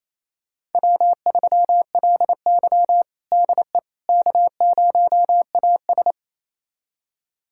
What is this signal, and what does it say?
Morse code, heard as a keyed tone, about 28 wpm, 700 Hz. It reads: W3LY DE K0AS
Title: USA Callsign DE Callsign - 4 Characters - 28wpm